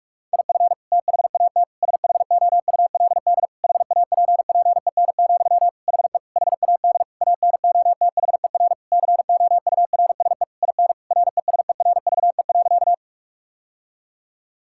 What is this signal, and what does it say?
Morse code, heard as a keyed tone, about 46 wpm, 700 Hz. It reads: IF THAT SHOULD HAPPEN, HE HAD ANOTHER COURSE IN RESERVE.
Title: Sentences from Top 1000 Words - Review Part 6 - 45wpm